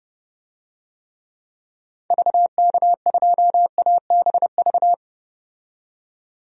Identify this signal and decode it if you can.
Morse code, heard as a keyed tone, about 30 wpm, 700 Hz. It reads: VK2ABV
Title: DXCC-100 Call Signs - 6 Characters - 30wpm